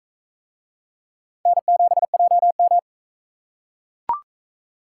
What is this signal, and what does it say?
Morse code, heard as a keyed tone, about 40 wpm, 700 Hz. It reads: N7JM